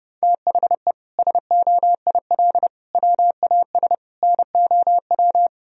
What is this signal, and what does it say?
Morse code, heard as a keyed tone, about 32 wpm, 700 Hz. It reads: THE SOIL WAS NOW
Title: Sentences from Top 1000 Words - Review Part 4 - 30wpm